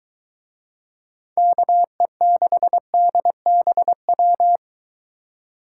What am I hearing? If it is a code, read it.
Morse code, heard as a keyed tone, about 22 wpm, 700 Hz. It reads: KE6DBW